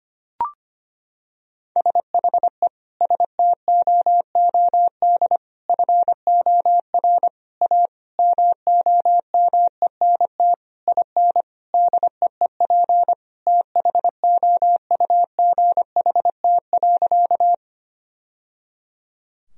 Morse code, heard as a keyed tone, about 25 wpm, 700 Hz. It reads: SHE STOOD FOR A MOMENT IN DEEP THOUGHT.
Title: Sentences from Top 1000 Words - Review Part 3 - 25wpm